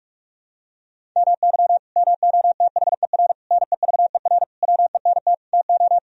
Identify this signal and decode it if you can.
Morse code, heard as a keyed tone, about 43 wpm, 700 Hz. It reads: MY MOTHER NEVER WENT TO
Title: Sentences from Top 1000 Words - Review Part 1 - 45wpm